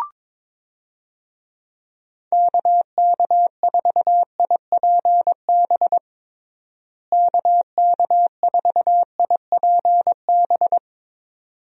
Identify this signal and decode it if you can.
Morse code, heard as a keyed tone, about 22 wpm, 700 Hz. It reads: KK4IPB KK4IPB
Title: USA Call Signs - 6 Characters - Repeated 2x - 22wpm